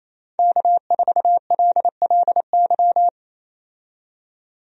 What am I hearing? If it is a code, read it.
Morse code, heard as a keyed tone, about 27 wpm, 700 Hz. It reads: K4LLY
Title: USA Call Signs - 5 Characters - 28wpm